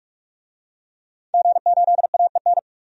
Morse code, heard as a keyed tone, about 45 wpm, 700 Hz. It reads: G8AEN
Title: DXCC-100 Call Signs - 5 Characters - 45wpm